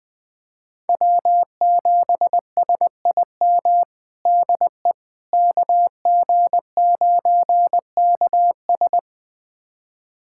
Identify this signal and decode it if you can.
Morse code, heard as a keyed tone, about 20 wpm, 700 Hz. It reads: W7SIM DE KG9KS
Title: USA Callsign DE Callsign - 5 Characters - 20wpm